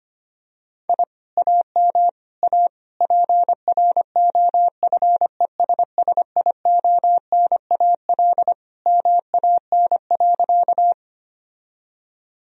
Morse code, heard as a keyed tone, about 25 wpm, 700 Hz. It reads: I AM A PROFESSIONAL MAN.
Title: Sentences from Top 1000 Words - Review Part 4 - 25wpm